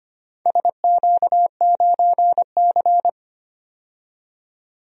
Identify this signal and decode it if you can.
Morse code, heard as a keyed tone, about 25 wpm, 700 Hz. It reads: SQ9C
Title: DXCC-100 Call Signs - 4 Characters - 25wpm